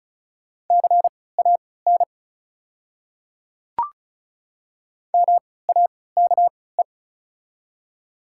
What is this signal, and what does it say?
Morse code, heard as a keyed tone, about 34 wpm, spaced as for 20 wpm, 690 Hz. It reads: CAN MAKE
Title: 3x Character Spacing - Top 100 Words - 35wpm